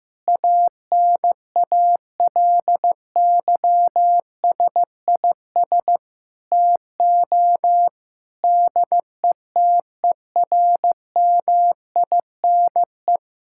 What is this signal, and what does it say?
Morse code, heard as a keyed tone, about 16 wpm, 700 Hz. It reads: ANALYSIS TO DETERMINE
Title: Sentences from Top 1000 Words - Review Part 5 - 15wpm